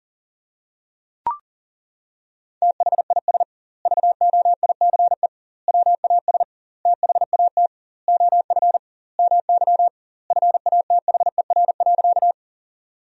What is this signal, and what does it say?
Morse code, heard as a keyed tone, about 40 wpm, 700 Hz. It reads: THIS VOICE WAS THAT OF MY FATHER.
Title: 2x Word-Spacing - Sentences from Top 500 Words - 40wpm